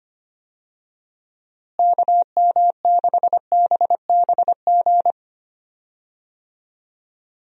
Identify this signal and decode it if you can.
Morse code, heard as a keyed tone, about 25 wpm, 700 Hz. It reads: KM6BBG
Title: USA Call Signs - 6 Characters - Repeated 2x - 25wpm